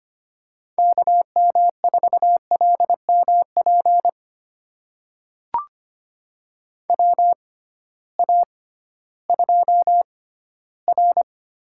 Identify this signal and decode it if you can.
Morse code, heard as a keyed tone, about 25 wpm, spaced as for 14 wpm, 700 Hz. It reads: KM4LMP WA2R
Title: USA Call Signs - 6 Characters - 6x Character Spacing - 25wpm